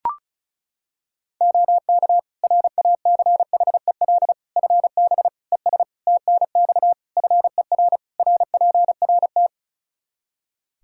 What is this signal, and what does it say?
Morse code, heard as a keyed tone, about 36 wpm, 700 Hz. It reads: OK RACHEL FB ES TNX FER RPRT
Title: Practice QSO - Exchange 2 - Info - 35wpm